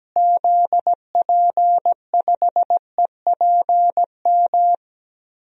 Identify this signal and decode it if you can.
Morse code, heard as a keyed tone, about 17 wpm, 700 Hz. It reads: ZP5EPM
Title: DXCC-100 Call Signs - 6 Characters - Repeated 2x - 17wpm